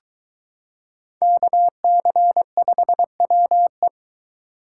Morse code, heard as a keyed tone, about 22 wpm, 700 Hz. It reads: KC5WE